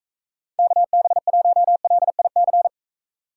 Morse code, heard as a keyed tone, about 34 wpm, 700 Hz. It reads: KB1LIC
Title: Speed Racing - USA 2x3 Call Signs 28wpm